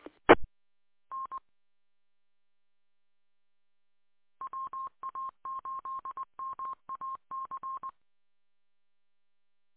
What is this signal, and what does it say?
Morse code, heard as a keyed tone, about 20 wpm, 1060 Hz. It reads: E N WA8MAC